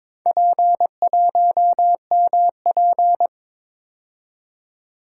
Morse code, heard as a keyed tone, about 22 wpm, 700 Hz. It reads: P1MP
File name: practice / 22wpm / Four Letter-Number Combinations - 22wpm